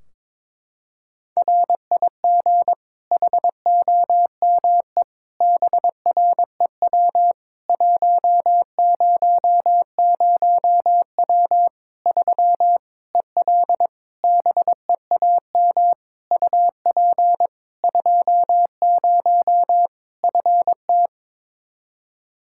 Morse code, heard as a keyed tone, about 22 wpm, 700 Hz. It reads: RIG HOME BREW 100W 3 EL BEAM UP 20 FT
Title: Practice QSO - Exchange 2 - Info - 22wpm